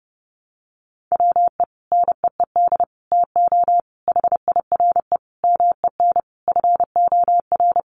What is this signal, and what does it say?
Morse code, heard as a keyed tone, about 30 wpm, 700 Hz. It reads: WE NEED TO HIRE MEN FOR
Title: Sentences from Top 1400 Words 30wpm